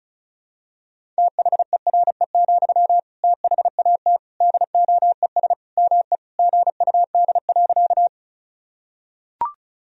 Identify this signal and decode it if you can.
Morse code, heard as a keyed tone, about 34 wpm, 700 Hz. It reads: THERE, THAT DOES ME GUD.
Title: Sentences from Top 200 Words and Abbreviations 35wpm